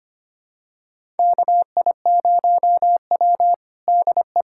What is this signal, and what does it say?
Morse code, heard as a keyed tone, about 25 wpm, 700 Hz. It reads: KI0W DE
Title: USA Callsign DE Callsign - 4 Characters - 25wpm